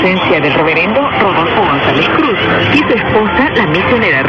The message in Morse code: none

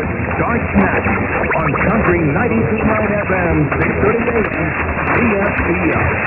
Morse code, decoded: none

